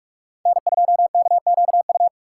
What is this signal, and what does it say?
Morse code, heard as a keyed tone, about 41 wpm, 700 Hz. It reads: N2KQU